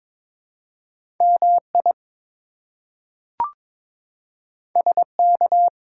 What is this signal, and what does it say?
Morse code, heard as a keyed tone, about 22 wpm, 700 Hz. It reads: MI SK
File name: ARRL Sections v2 - 22wpm